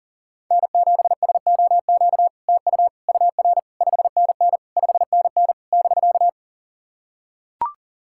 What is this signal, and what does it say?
Morse code, heard as a keyed tone, about 40 wpm, 690 Hz. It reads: N7SOQ TU UR 5NN 5NN <BK>
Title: SOTA QSO Practice 40wpm